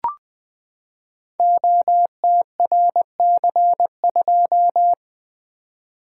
Morse code, heard as a keyed tone, about 20 wpm, 700 Hz. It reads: OTRC2
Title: Code Groups - 5 Letter:Numbers - 20wpm